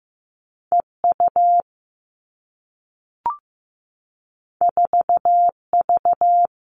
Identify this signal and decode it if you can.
Morse code, heard as a keyed tone, about 15 wpm, 700 Hz. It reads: E U 4 V
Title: Two Challenging Letter-Number Combinations 15wpm